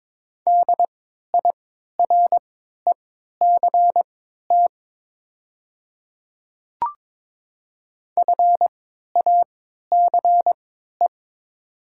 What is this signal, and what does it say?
Morse code, heard as a keyed tone, about 22 wpm, spaced as for 12 wpm, 700 Hz. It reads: DIRECT FACE
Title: 3x Character Spacing - Top 500 Words - 22wpm